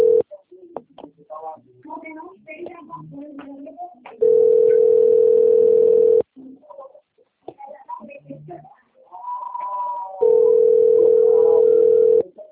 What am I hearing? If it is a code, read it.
Ringback tone (North America), 440 + 480 Hz.
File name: out- - - - - .36072